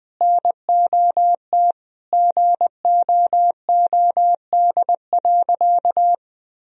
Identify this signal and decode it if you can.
Morse code, heard as a keyed tone, about 21 wpm, 700 Hz. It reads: NOT GOOD.
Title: Sentences from Top 1000 Words - Review Part 1 - 20wpm